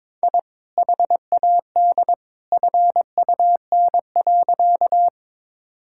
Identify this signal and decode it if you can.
Morse code, heard as a keyed tone, about 21 wpm, 700 Hz. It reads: I HAD FUN.
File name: practice / 22wpm / Sentences with 2-Word Phrases - 22wpm